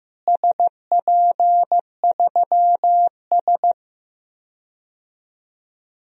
Morse code, heard as a keyed tone, about 15 wpm, 700 Hz. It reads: SP3S
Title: DXCC-100 Call Signs - 4 Characters - Repeated 2x - 15wpm